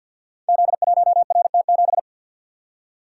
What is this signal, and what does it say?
Morse code, heard as a keyed tone, about 52 wpm, 690 Hz. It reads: Z1RT7
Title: Code Groups - 5 Letter:Numbers - 50wpm